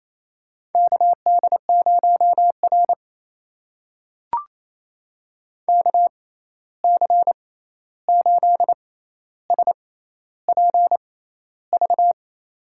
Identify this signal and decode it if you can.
Morse code, heard as a keyed tone, about 28 wpm, spaced as for 13 wpm, 700 Hz. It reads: KD0R KC8SPV
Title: USA Call Signs - 6x Character Spacing - 28wpm